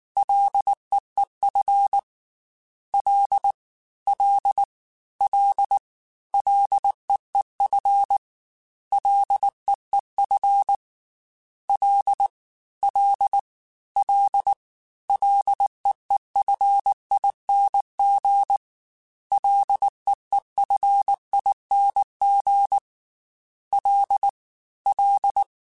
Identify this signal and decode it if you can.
Morse code, heard as a keyed tone, about 19 wpm, 800 Hz. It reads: LEEF L L L LEEF LEEF L L L LEEFING LEEFING L L
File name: F and L / F and L mp3 / F and L Prefixes 3x2 Twice @18